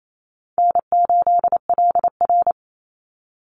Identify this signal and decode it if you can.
Morse code, heard as a keyed tone, about 28 wpm, 700 Hz. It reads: N8LR
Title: Easy Call Signs - 28wpm